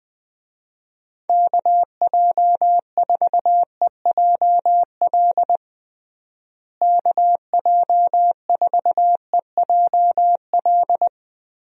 Morse code, heard as a keyed tone, about 21 wpm, 700 Hz. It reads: KJ4EJL KJ4EJL
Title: USA Call Signs - 6 Characters - Repeated 2x - 20wpm